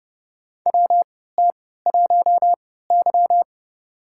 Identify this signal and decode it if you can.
Morse code, heard as a keyed tone, about 30 wpm, spaced as for 17 wpm, 700 Hz. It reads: WT1Y
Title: USA Call Signs - 3x Character Spacing - 30wpm